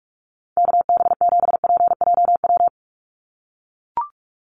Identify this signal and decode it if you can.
Morse code, heard as a keyed tone, about 44 wpm, 700 Hz. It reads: KB7PJW
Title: USA Call Signs 45wpm